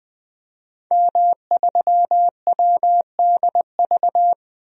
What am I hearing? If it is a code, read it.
Morse code, heard as a keyed tone, about 20 wpm, 700 Hz. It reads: M3WDV
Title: DXCC-100 Call Signs - 5 Characters - 20wpm